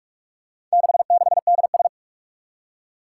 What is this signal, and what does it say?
Morse code, heard as a keyed tone, about 44 wpm, 700 Hz. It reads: 66DS
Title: Four Challenging Letter-Number Combinations 45wpm